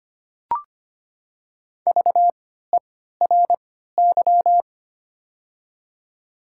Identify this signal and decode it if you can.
Morse code, heard as a keyed tone, about 25 wpm, spaced as for 14 wpm, 700 Hz. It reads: VERY
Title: 3x Character Spacing - Top 500 Words - 25wpm